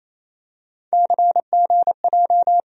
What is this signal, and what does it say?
Morse code, heard as a keyed tone, about 28 wpm, 700 Hz. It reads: CGJ
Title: USA Call Sign Postfixes - 3 Letters 28wpm